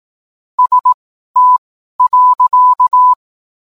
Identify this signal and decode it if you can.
Morse code, heard as a keyed tone, about 18 wpm, spaced as for 12 wpm, 1000 Hz. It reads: ST.